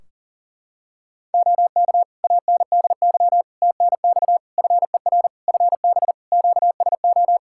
Morse code, heard as a keyed tone, about 40 wpm, 700 Hz. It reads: OK ANDY TNX FER FB QSO
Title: K1USN SST QSO Practice 40wpm